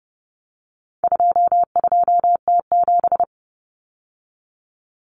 Morse code, heard as a keyed tone, about 30 wpm, 700 Hz. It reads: 22T7
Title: Common Cut Numbers - 4-Digit 30wpm